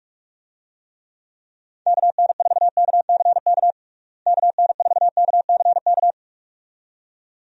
Morse code, heard as a keyed tone, about 45 wpm, 700 Hz. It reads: KN4KCK KN4KCK
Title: USA Call Signs - 6 Characters - Repeated 2x - 45wpm